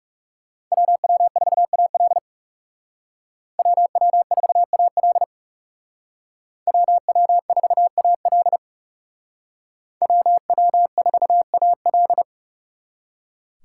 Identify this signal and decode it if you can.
Morse code, heard as a keyed tone, about 35 wpm, 700 Hz. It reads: WW4AL WW4AL WW4AL WW4AL
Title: Speed Racing - USA 2x2 Call Signs 30wpm